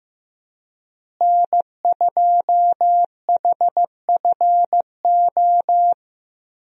Morse code, heard as a keyed tone, about 15 wpm, 700 Hz. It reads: N2HFO
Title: USA Call Signs - 5 Characters - Repeated 2x - 15wpm